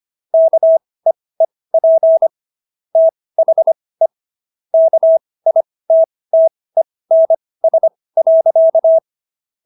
Morse code, heard as a keyed tone, about 25 wpm, spaced as for 18 wpm, 650 Hz. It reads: KEEP THE KITTENS.